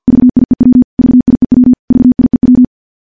Morse code, heard as a keyed tone, about 50 wpm, 250 Hz. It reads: VIEW VIEW VIEW
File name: morse / morse-training-50WPM